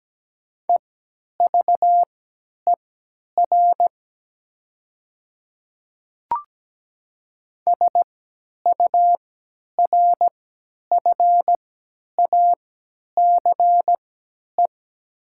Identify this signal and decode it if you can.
Morse code, heard as a keyed tone, about 17 wpm, spaced as for 10 wpm, 700 Hz. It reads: EVER SURFACE
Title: 3x Character Spacing - Top 500 Words - 17wpm